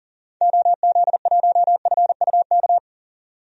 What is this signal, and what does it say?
Morse code, heard as a keyed tone, about 40 wpm, 700 Hz. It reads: OZ1FUK
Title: DXCC-100 Call Signs - 6 Characters - 40wpm